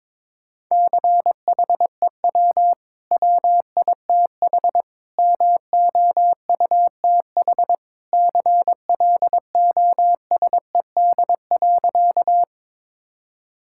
Morse code, heard as a keyed tone, about 22 wpm, 690 Hz. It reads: CHEW WITH MOUTH CLOSED.
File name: practice / 22wpm / Easy Sentences with Hints - 22wpm